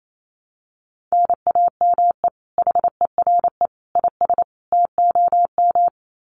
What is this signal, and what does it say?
Morse code, heard as a keyed tone, about 28 wpm, 700 Hz. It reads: NAME HERE IS TOM
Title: Common QSO Phrases 28wpm